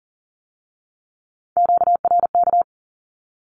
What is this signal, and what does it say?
Morse code, heard as a keyed tone, about 38 wpm, 700 Hz. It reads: QRK
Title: Q-Codes with Spoken Meaning - 40wpm